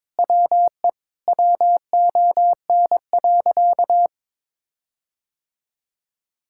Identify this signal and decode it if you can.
Morse code, heard as a keyed tone, about 22 wpm, 700 Hz. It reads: WE WON.
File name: practice / 22wpm / Sentences with 8-Word Phrases - 22wpm